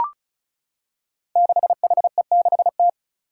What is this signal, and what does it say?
Morse code, heard as a keyed tone, about 36 wpm, 700 Hz. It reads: 6HE6T
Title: Five Challenging Letter-Number Combinations 35wpm